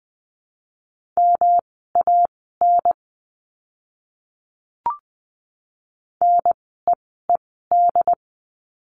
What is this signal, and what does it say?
Morse code, heard as a keyed tone, about 20 wpm, spaced as for 15 wpm, 700 Hz. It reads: MAN NEED
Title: 206 Lesson 33f - Letter j - Words 20wpm